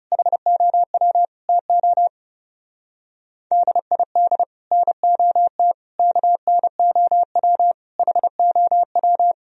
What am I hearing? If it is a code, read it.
Morse code, heard as a keyed tone, about 35 wpm, 700 Hz. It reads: HOW TO DID NOT KNOW HOW
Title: Speed-Racing - Sets of 5 Words 30wpm